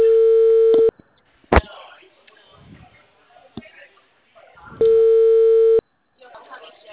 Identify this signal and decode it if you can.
Ringback tone (China, 450 Hz), with speech.